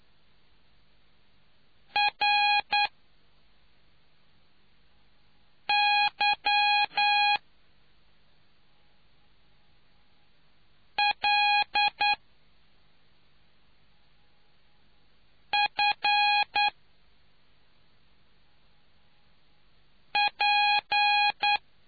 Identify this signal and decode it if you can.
Morse code, heard as a keyed tone, about 9 wpm, spaced as for 2 wpm, 830 Hz. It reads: RYLFP